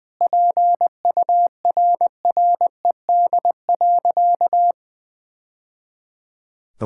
Morse code, heard as a keyed tone, about 21 wpm, 700 Hz. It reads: PURRED.
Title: Sentences with 2-Word Phrases - 20wpm